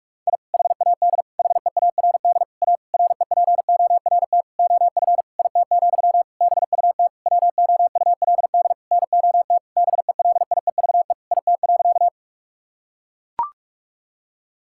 Morse code, heard as a keyed tone, about 41 wpm, 700 Hz. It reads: I HAD HEARD A REPORT OF IT, BUT WOULD NOT BELIEVE IT.
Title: Sentences from Top 1000 Words - Review Part 6 - 45wpm